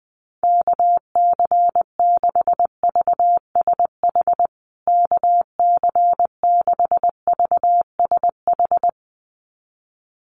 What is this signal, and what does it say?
Morse code, heard as a keyed tone, about 20 wpm, 700 Hz. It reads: KC6VSH KC6VSH
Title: USA Call Signs Repeated 2x 20wpm